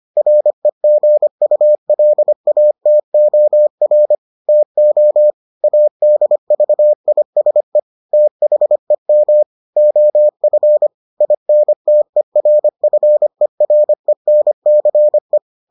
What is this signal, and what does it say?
Morse code, heard as a keyed tone, about 25 wpm, 600 Hz. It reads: REGULATOR TO ADVISE THEM OF INTERFERENCE